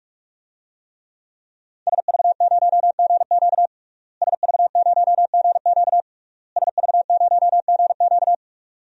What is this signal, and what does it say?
Morse code, heard as a keyed tone, about 46 wpm, 700 Hz. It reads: SV0GQ SV0GQ SV0GQ
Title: DXCC-100 Call Signs - 5 Characters - Repeated 3x - 45wpm